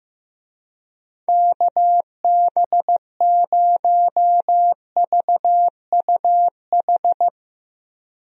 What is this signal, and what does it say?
Morse code, heard as a keyed tone, about 15 wpm, 700 Hz. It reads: KB0VUH
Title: USA Call Signs - 6 Characters - 15wpm